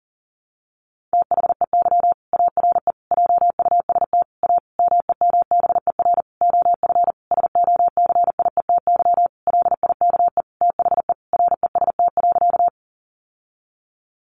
Morse code, heard as a keyed tone, about 40 wpm, 700 Hz. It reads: THEY ARE JUST A MEMBER OF SOCIETY LIKE THE REST.
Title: Sentences from Top 1100 Words 40wpm